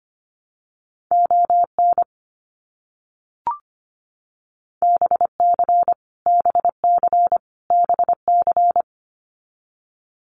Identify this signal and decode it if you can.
Morse code, heard as a keyed tone, about 25 wpm, 700 Hz. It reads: ON BC BC BC
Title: Canadian Provinces and Territories - Repeated 3x - 25wpm